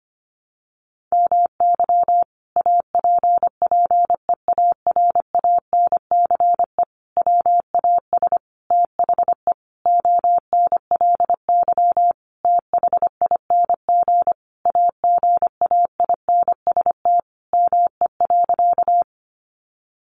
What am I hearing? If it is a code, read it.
Morse code, heard as a keyed tone, about 25 wpm, 700 Hz. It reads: MY APPEARANCE WAS THE ONLY THING AGAINST ME.
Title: Sentences from Top 1100 Words 25wpm